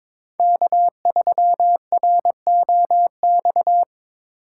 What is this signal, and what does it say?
Morse code, heard as a keyed tone, about 21 wpm, 700 Hz. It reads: K3ROX
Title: USA Call Signs - 5 Characters - 4x Character Spacing - 22wpm